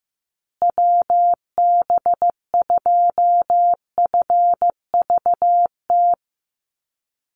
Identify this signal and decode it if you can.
Morse code, heard as a keyed tone, about 15 wpm, 700 Hz. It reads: WB2FVT
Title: USA Call Signs Repeated 3x 15wpm